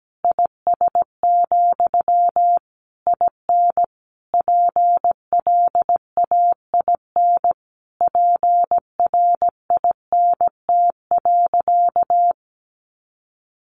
Morse code, heard as a keyed tone, about 17 wpm, 700 Hz. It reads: IS, IN PLAIN PRINT.
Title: Sentences from Top 1400 Words 17wpm